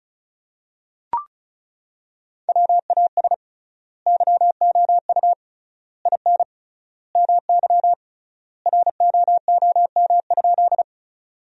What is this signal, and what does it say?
Morse code, heard as a keyed tone, about 36 wpm, 700 Hz. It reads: WAS YOU IN MY ROOM?
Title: 3x Word-Spacing - Sentences from Top 300 Words - 35wpm